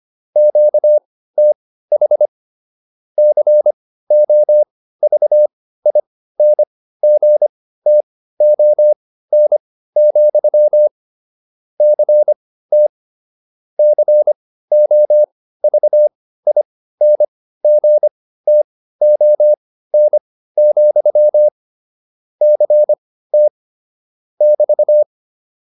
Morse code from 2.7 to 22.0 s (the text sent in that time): COVINGTON, CT COVINGTON,